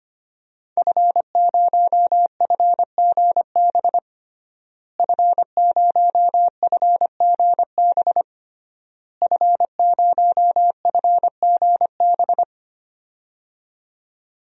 Morse code, heard as a keyed tone, about 25 wpm, 700 Hz. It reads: F0FGB F0FGB F0FGB
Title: DXCC-100 Call Signs - 5 Characters - Repeated 3x - 25wpm